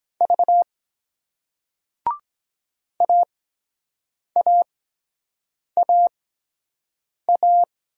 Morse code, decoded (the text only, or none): V AAAA